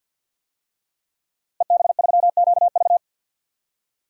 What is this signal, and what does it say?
Morse code, heard as a keyed tone, about 48 wpm, 700 Hz. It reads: EB3QV
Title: DXCC-100 Call Signs - 5 Characters - Repeated 3x - 50wpm